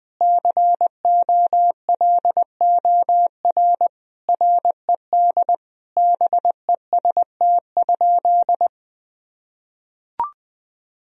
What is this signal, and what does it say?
Morse code, heard as a keyed tone, about 20 wpm, 700 Hz. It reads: COLOR RED BEST?
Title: Sentences from Top 1000 Words - Review Part 2 - 20wpm